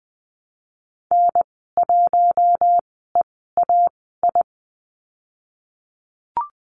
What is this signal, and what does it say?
Morse code, heard as a keyed tone, about 20 wpm, spaced as for 14 wpm, 700 Hz. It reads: N1EAI
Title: 042 Lesson 8g - Number 1 - Call Signs 20wpm